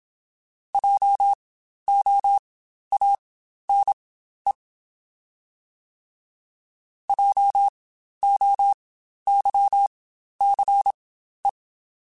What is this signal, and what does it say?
Morse code, heard as a keyed tone, about 27 wpm, spaced as for 12 wpm, 800 Hz. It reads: JOANE JOYCE
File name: Spaced Alphas / Spaced Alphas mp3s / Js sp @25